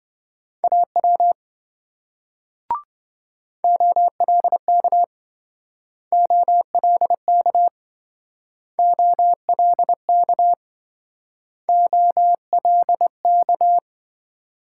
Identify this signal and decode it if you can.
Morse code, heard as a keyed tone, about 26 wpm, 700 Hz. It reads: AW OLK OLK OLK OLK